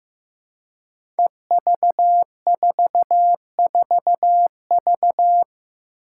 Morse code, heard as a keyed tone, about 15 wpm, 700 Hz. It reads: EV44V
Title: Five Challenging Letter-Number Combinations 15wpm